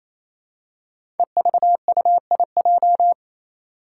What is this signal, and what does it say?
Morse code, heard as a keyed tone, about 29 wpm, 700 Hz. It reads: EVUIJ
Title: Five Challenging Letter-Number Combinations 28wpm